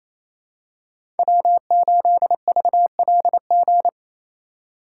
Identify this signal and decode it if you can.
Morse code, heard as a keyed tone, about 28 wpm, 700 Hz. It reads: W8VLG
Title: USA Call Signs - 5 Characters - 2x Character Spacing - 28wpm